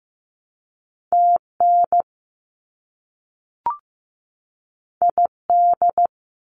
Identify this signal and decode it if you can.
Morse code, heard as a keyed tone, about 15 wpm, 700 Hz. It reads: TN ID
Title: US States 15wpm